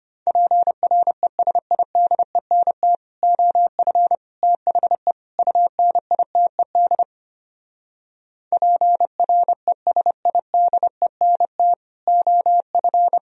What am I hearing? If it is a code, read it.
Morse code, heard as a keyed tone, about 28 wpm, 700 Hz. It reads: PRESIDENT OF THE UNITED PRESIDENT OF
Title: Speed-Racing - Sets of 4 Words 25wpm